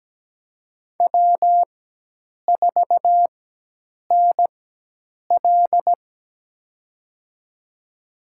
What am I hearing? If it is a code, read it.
Morse code, heard as a keyed tone, about 17 wpm, spaced as for 8 wpm, 700 Hz. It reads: W4NL
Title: USA Call Signs - 4 Characters - 4x Character Spacing - 17wpm